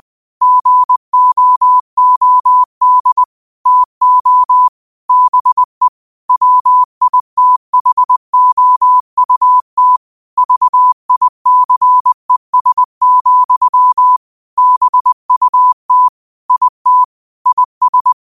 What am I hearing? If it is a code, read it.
Morse code, heard as a keyed tone, about 20 wpm, 1000 Hz. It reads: GOOD TO BE WITHOUT VICES, BUT IT IS
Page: New quotes every day in morse code at 20 Words per minute.